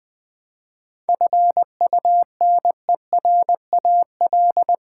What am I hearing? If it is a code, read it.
Morse code, heard as a keyed tone, about 20 wpm, 700 Hz. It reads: FUNERAL
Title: Seven Letter Words - 20wpm